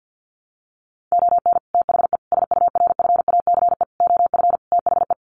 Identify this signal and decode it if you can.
Morse code, heard as a keyed tone, about 48 wpm, 700 Hz. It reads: ON THE SURFACE OF THE
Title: Sets of 5 Words Encore 50wpm